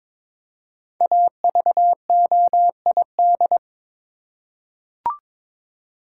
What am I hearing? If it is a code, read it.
Morse code, heard as a keyed tone, about 22 wpm, 700 Hz. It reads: AVOID
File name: 5 Letter Words - First 3 Letters Given - 22wpm